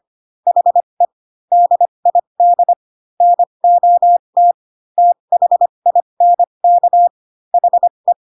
Morse code, heard as a keyed tone, about 25 wpm, spaced as for 22 wpm, 700 Hz. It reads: HE DID NOT THINK HE
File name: Pinocchio / Pinocchio ch4 18 25 ik1hge